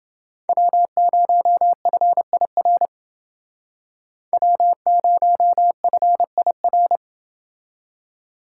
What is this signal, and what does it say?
Morse code, heard as a keyed tone, about 28 wpm, 700 Hz. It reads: W0FIR W0FIR